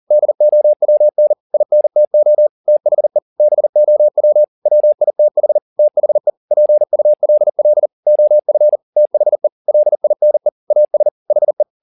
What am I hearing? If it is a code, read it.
Morse code, heard as a keyed tone, about 40 wpm, 600 Hz. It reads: DOWN INTO THE BOW WITH THE PULL OF THE LINE AS HE
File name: the old man and the sea 2 40wpm